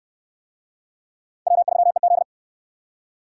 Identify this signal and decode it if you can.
Morse code, heard as a keyed tone, about 67 wpm, 700 Hz. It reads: W3EZ